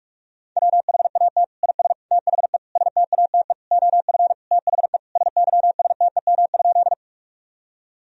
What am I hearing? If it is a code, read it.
Morse code, heard as a keyed tone, about 46 wpm, 700 Hz. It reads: WHAT IS THE STATE OF THE SYSTEM?
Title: Sentences from Top 1000 Words - Review Part 3 - 45wpm